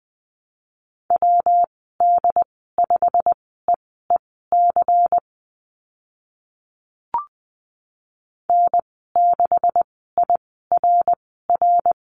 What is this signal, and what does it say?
Morse code, heard as a keyed tone, about 20 wpm, spaced as for 14 wpm, 700 Hz. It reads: WD5EEC N6IRR
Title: 140 Lesson 23g - Letter f - Call Signs 20wpm